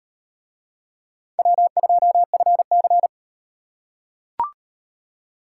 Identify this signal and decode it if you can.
Morse code, heard as a keyed tone, about 38 wpm, 700 Hz. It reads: W2FC